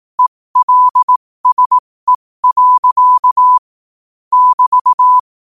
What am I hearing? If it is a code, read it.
Morse code, heard as a keyed tone, about 18 wpm, spaced as for 15 wpm, 1000 Hz. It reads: ELSE. <BT>
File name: qotdfiles / 15wpm